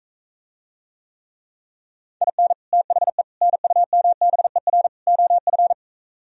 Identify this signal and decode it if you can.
Morse code, heard as a keyed tone, about 45 wpm, 700 Hz. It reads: IN THE NUMBER OF